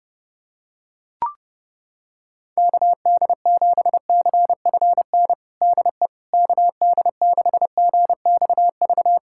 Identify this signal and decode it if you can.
Morse code, heard as a keyed tone, about 32 wpm, 700 Hz. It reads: KD7CFN DE KD6GXV
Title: USA Callsign DE Callsign 30wpm